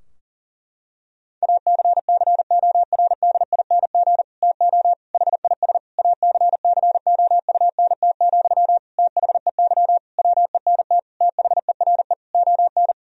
Morse code, heard as a keyed tone, about 40 wpm, 700 Hz. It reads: ACCORDING TO HIS ACCOUNT, THEY WENT THERE ON
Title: Sentences from Top 1000 Words - Review Part 4 - 40wpm